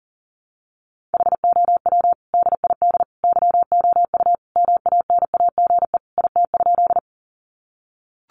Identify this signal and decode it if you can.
Morse code, heard as a keyed tone, about 40 wpm, 700 Hz. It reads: HOW DID YOU MANAGE IT?
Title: Sentences from Top 1400 Words 40wpm